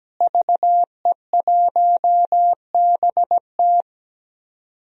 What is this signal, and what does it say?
Morse code, heard as a keyed tone, about 17 wpm, 700 Hz. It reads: VE1BT
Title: DXCC-100 Call Signs - 5 Characters - Repeated 2x - 17wpm